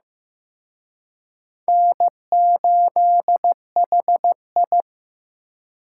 Morse code, heard as a keyed tone, about 18 wpm, 700 Hz. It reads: N8HI
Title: Speed Racing - USA 1x2 Call Signs 15wpm